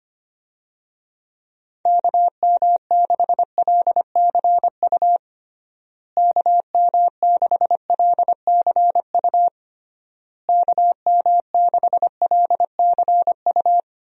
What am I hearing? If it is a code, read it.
Morse code, heard as a keyed tone, about 25 wpm, 700 Hz. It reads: KM6LCU KM6LCU KM6LCU
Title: USA Call Signs - 6 Characters - Repeated 3x - 25wpm